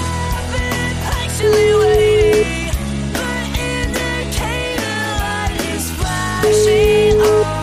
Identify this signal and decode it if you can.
Ringback tone (Europe, 425 Hz), with noise.